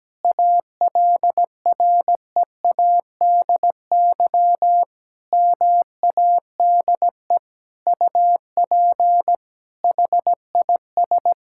Morse code, heard as a keyed tone, about 17 wpm, 700 Hz. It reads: ALREADY MADE UP HIS
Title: Sentences from Top 1000 Words - Review Part 5 - 17wpm